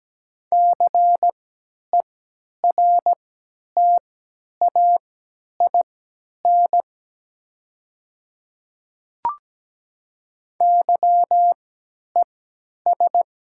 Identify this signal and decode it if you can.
Morse code, heard as a keyed tone, about 17 wpm, spaced as for 10 wpm, 700 Hz. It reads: CERTAIN YES
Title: 3x Character Spacing - Top 500 Words - 17wpm